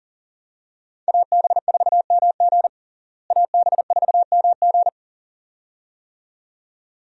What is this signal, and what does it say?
Morse code, heard as a keyed tone, about 37 wpm, 700 Hz. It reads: AB4MG AB4MG
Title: USA Call Signs - 5 Characters - Repeated 2x - 40wpm